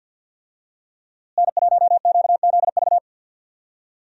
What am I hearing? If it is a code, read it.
Morse code, heard as a keyed tone, about 46 wpm, 700 Hz. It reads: N1QZV
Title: USA Call Signs - 5 Characters - Repeated 2x - 50wpm